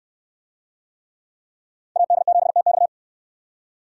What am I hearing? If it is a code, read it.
Morse code, heard as a keyed tone, about 60 wpm, 700 Hz. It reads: AD7TX